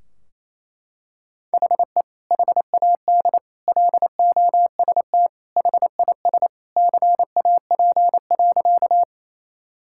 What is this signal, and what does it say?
Morse code, heard as a keyed tone, about 28 wpm, 700 Hz. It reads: HE HAD LOST HIS CAP.
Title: Sentences from Top 1600 Words 28wpm